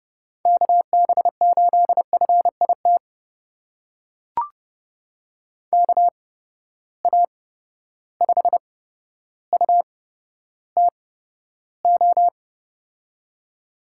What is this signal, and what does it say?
Morse code, heard as a keyed tone, about 30 wpm, spaced as for 13 wpm, 700 Hz. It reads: KB8FIT KA5UTO
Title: USA Call Signs - 8x Character Spacing - 30wpm